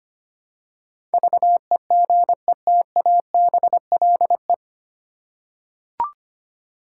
Morse code, heard as a keyed tone, about 25 wpm, 700 Hz. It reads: VEGETABLE
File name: Nine Letter Words - 25wpm